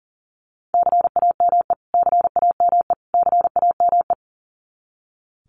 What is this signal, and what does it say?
Morse code, heard as a keyed tone, about 40 wpm, 700 Hz. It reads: CAME CAME CAME
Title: Top 300 Words Repeated 40wpm